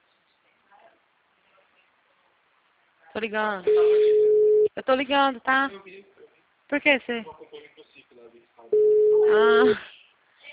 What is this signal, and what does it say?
Ringback tone (Europe, 425 Hz), with speech.
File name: out- -2009- - - .68372